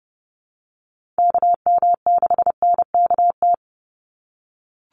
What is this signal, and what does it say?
Morse code, heard as a keyed tone, about 30 wpm, 700 Hz. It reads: KM6NKT
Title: USA Call Signs 30wpm